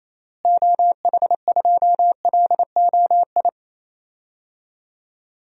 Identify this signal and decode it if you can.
Morse code, heard as a keyed tone, about 28 wpm, 700 Hz. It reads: OH2LOI
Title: DXCC-100 Call Signs - 6 Characters - Repeated 3x - 28wpm